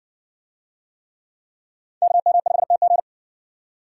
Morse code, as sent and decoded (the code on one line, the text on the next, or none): -.- -- ..... - --.
KM5TG